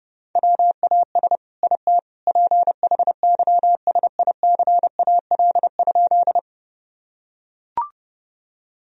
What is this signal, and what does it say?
Morse code, heard as a keyed tone, about 32 wpm, 700 Hz. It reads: WAS IT PHYSICAL?
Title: Sentences from Top 1000 Words - Review Part 4 - 30wpm